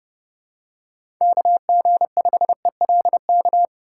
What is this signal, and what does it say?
Morse code, heard as a keyed tone, about 31 wpm, 700 Hz. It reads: KG5ELK
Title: USA Call Signs - 6 Characters - 8x Character Spacing - 30wpm